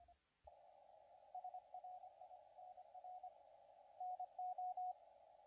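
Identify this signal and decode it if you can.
Morse code, heard as a keyed tone, about 27 wpm, 720 Hz, with radio noise. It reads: SRI NR NO
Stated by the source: CW contest audio recorded at LA4C during the SAC CW contest.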